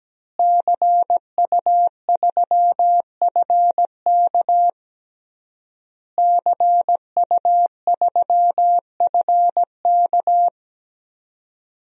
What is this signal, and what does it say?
Morse code, heard as a keyed tone, about 17 wpm, 700 Hz. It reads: CU3FK CU3FK
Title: DXCC-100 Call Signs - 5 Characters - Repeated 3x - 17wpm